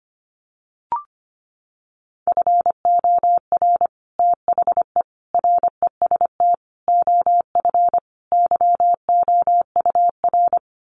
Morse code, heard as a keyed tone, about 25 wpm, 700 Hz. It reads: FOR THE REST OF YOUR
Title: Sets of 5 Words Encore 25wpm